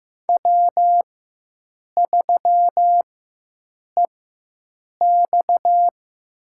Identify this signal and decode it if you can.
Morse code, heard as a keyed tone, about 15 wpm, spaced as for 7 wpm, 700 Hz. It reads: W3EX
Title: USA Call Signs - 4 Characters - 4x Character Spacing - 15wpm